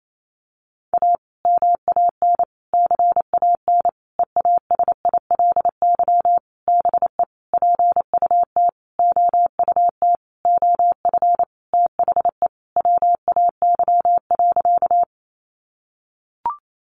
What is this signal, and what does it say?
Morse code, heard as a keyed tone, about 28 wpm, 700 Hz. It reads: A MAN CAN EASILY BE PUT OUT OF THE WAY.
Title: Sentences from Top 900 Words 28wpm